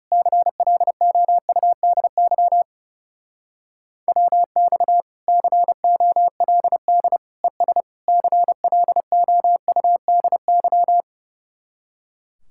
Morse code, heard as a keyed tone, about 38 wpm, 700 Hz. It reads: CLOUDY WX COLD ES CLOUDY
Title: Speed-Racing Common QSO Phrases - 30wpm